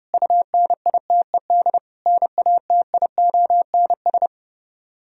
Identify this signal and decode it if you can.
Morse code, heard as a keyed tone, about 33 wpm, 700 Hz. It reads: UNITED NATIONS